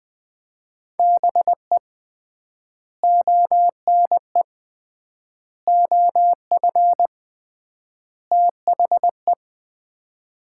Morse code, heard as a keyed tone, about 21 wpm, 700 Hz. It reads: BE ONE OF THE
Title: 3x Word-Spacing - Sets of 4 Words - 20wpm